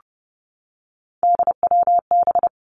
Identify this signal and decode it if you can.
Morse code, heard as a keyed tone, about 30 wpm, 700 Hz. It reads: DWB
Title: Three Challenging Letter-Number Combinations 30wpm